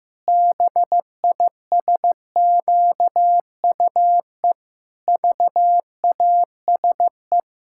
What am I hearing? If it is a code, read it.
Morse code, heard as a keyed tone, about 15 wpm, 700 Hz. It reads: BISQUE VASE
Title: Pangram Challenge v2 - 15wpm